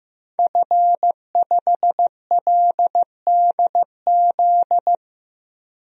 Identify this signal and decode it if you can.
Morse code, heard as a keyed tone, about 15 wpm, 700 Hz. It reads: F5LDZ
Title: DXCC-100 Call Signs - 5 Characters - 15wpm